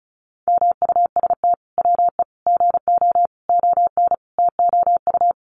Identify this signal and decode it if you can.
Morse code, heard as a keyed tone, about 34 wpm, 700 Hz. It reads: MUST WE GO ON TOU
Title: Sentences from Top 1400 Words 35wpm